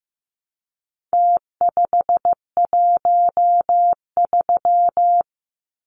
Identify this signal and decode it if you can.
Morse code, heard as a keyed tone, about 15 wpm, 700 Hz. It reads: T513
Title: Numbers - Contest Style - 4-Digit 15wpm